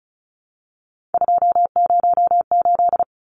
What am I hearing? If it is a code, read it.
Morse code, heard as a keyed tone, about 34 wpm, 700 Hz. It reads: 208
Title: Numbers - 3-Digit 35wpm